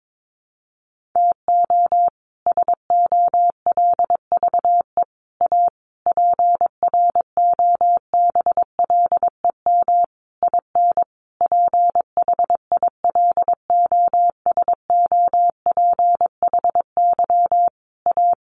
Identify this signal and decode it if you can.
Morse code, heard as a keyed tone, about 22 wpm, 700 Hz. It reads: TO SOLVE A PROBLEM IN PHILOSOPHY A
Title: Sentences from Top 1400 Words 22wpm